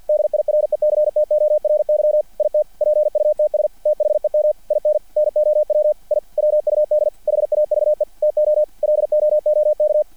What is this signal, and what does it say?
Morse code, heard as a keyed tone, about 50 wpm, 600 Hz. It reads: DIRECTORY IT PUTS THEM IN NOW I WUD HAVE TO LOOK
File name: archived / CRSnet